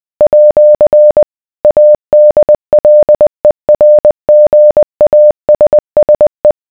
Morse code, heard as a keyed tone, about 20 wpm, 600 Hz. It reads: @ ADLERGASSE